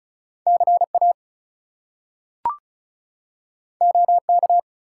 Morse code, heard as a keyed tone, about 36 wpm, 700 Hz. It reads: CA OK
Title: US States and Canadian Provinces 35wpm